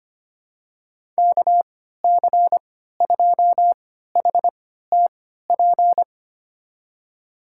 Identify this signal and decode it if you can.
Morse code, heard as a keyed tone, about 25 wpm, spaced as for 14 wpm, 700 Hz. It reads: KC2HTP
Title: USA Call Signs - 6 Characters - 3x Character Spacing - 25wpm